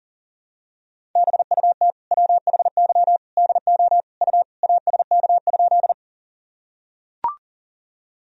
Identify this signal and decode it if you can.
Morse code, heard as a keyed tone, about 40 wpm, 700 Hz. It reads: BUT WHY DO U ASK?
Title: Sentences from Top 200 Words and Abbreviations 40wpm